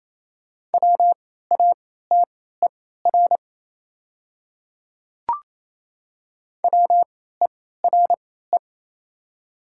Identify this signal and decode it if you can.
Morse code, heard as a keyed tone, about 28 wpm, spaced as for 16 wpm, 700 Hz. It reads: WATER WERE
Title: 3x Character Spacing - Top 200 Words - 28wpm